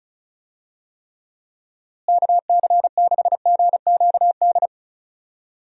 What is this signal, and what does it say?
Morse code, heard as a keyed tone, about 36 wpm, 700 Hz. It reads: KC6GQD
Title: USA Call Signs - 6 Characters - Repeated 2x - 35wpm